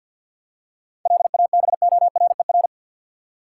Morse code, heard as a keyed tone, about 51 wpm, 700 Hz. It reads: LABORER
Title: Practice QSO - Exchange 3 - More Info - 50wpm